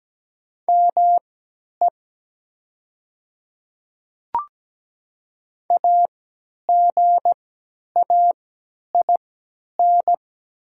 Morse code, heard as a keyed tone, about 17 wpm, spaced as for 10 wpm, 700 Hz. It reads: ME AGAIN
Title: 3x Character Spacing - Top 200 Words - 17wpm